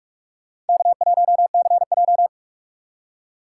Morse code, heard as a keyed tone, about 60 wpm, spaced as for 53 wpm, 700 Hz. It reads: K1CJ
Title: Speed Racing - USA 1x2 Call Signs 45wpm